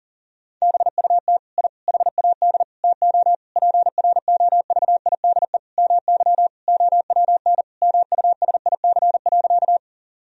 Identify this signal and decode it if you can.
Morse code, heard as a keyed tone, about 40 wpm, 700 Hz. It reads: BUT I HAD TO PROVIDE MY OWN MUSIC.
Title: Sentences from Top 1000 Words - Review Part 3 - 40wpm